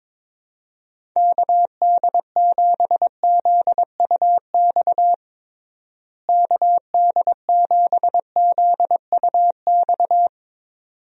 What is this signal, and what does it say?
Morse code, heard as a keyed tone, about 22 wpm, 700 Hz. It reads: KD7ZUX KD7ZUX
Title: USA Call Signs - 6 Characters - Repeated 2x - 22wpm